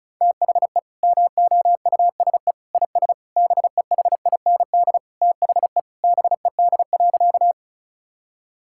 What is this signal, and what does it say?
Morse code, heard as a keyed tone, about 34 wpm, 700 Hz. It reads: THE MOUSE IS BEHIND THE BED.
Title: Sentences from Top 1000 Words - Review Part 5 - 35wpm